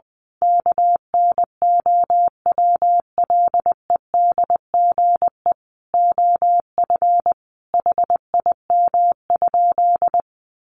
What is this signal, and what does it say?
Morse code, heard as a keyed tone, about 20 wpm, 700 Hz. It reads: KNOWLEDGE OF HIM?
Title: Sentences from Top 600 Words 20wpm